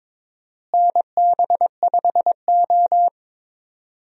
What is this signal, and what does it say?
Morse code, heard as a keyed tone, about 21 wpm, 700 Hz. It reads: NB5O
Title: USA Call Signs - 4 Characters - Repeated 3x - 22wpm